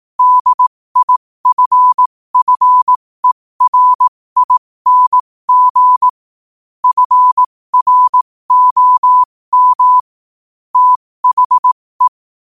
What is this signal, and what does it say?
Morse code, heard as a keyed tone, about 18 wpm, spaced as for 15 wpm, 1000 Hz. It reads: DIFFERING FROM THE
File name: qotdfiles / 15wpm